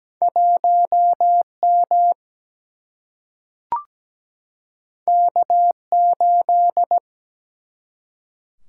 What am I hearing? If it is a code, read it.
Morse code, heard as a keyed tone, about 17 wpm, 700 Hz. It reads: 1M K8
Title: Single-Letter Single-Number 17wpm